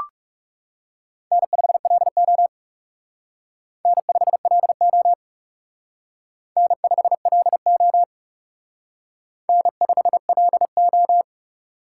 Morse code, heard as a keyed tone, about 37 wpm, 700 Hz. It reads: N5LO N5LO N5LO N5LO
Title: Speed Racing - USA 1x2 Call Signs 30wpm